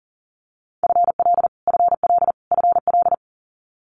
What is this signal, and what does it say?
Morse code, heard as a keyed tone, about 38 wpm, 700 Hz. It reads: FL FL FL
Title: Multi-letter Phonemes Repeated 3x 40wpm